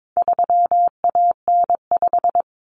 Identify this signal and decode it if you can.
Morse code, heard as a keyed tone, about 22 wpm, 700 Hz. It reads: 3AN5
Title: Common Cut Numbers - 4-Digit 22wpm